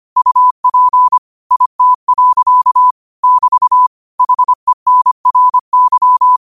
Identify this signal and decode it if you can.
Morse code, heard as a keyed tone, about 25 wpm, 1000 Hz. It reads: UP IT. <BT> HENRY